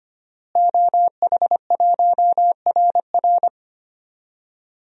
Morse code, heard as a keyed tone, about 25 wpm, 700 Hz. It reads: OH1RR
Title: DXCC-100 Call Signs - 5 Characters - Repeated 3x - 25wpm